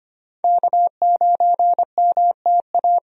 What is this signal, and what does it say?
Morse code, heard as a keyed tone, about 25 wpm, 700 Hz. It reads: K9MTA
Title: USA Call Signs - 5 Characters - Repeated 2x - 25wpm